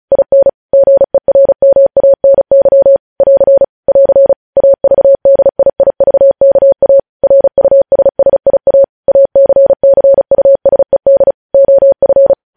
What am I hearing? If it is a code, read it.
Morse code, heard as a keyed tone, about 36 wpm, 550 Hz. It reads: IN GERMANY <AR> <AR> AVDIIVKA RUSSIA ACCUSED OF